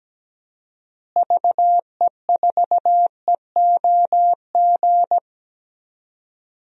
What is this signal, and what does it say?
Morse code, heard as a keyed tone, about 17 wpm, 700 Hz. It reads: VE4EOG
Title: DXCC-100 Call Signs - 6 Characters - Repeated 3x - 17wpm